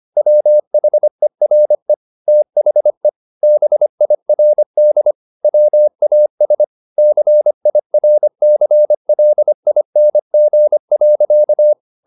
Morse code, heard as a keyed tone, about 25 wpm, 600 Hz. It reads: WHERE THE BIRD WAS CIRCLING.
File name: the old man and the sea part 1 25wpm